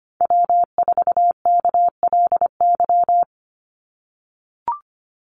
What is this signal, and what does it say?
Morse code, heard as a keyed tone, about 25 wpm, 700 Hz. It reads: W4KLY
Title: Call Signs - 25wpm